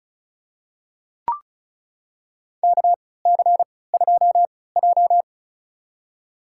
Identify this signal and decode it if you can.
Morse code, heard as a keyed tone, about 34 wpm, spaced as for 20 wpm, 700 Hz. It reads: KC2J
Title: USA Call Signs - 3x Character Spacing - 35wpm